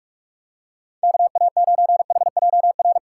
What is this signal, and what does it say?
Morse code, heard as a keyed tone, about 46 wpm, 700 Hz. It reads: KA9HJR
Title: USA Call Signs - 6 Characters - 2x Character Spacing - 45wpm